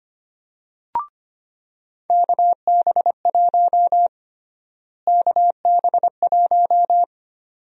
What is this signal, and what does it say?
Morse code, heard as a keyed tone, about 25 wpm, 700 Hz. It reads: KB1 KB1
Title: USA Call Sign Prefixes - Repeated 2x - 25wpm